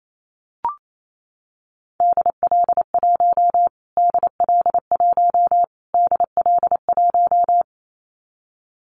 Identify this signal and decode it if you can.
Morse code, heard as a keyed tone, about 28 wpm, 700 Hz. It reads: DL1 DL1 DL1
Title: Callsign Prefixes for Top 25 DXCC - Repeated 3x - 28wpm